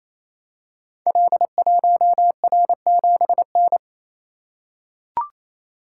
Morse code, heard as a keyed tone, about 27 wpm, 700 Hz. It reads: L1R7N
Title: Code Groups - 5 Letter:Numbers - 28wpm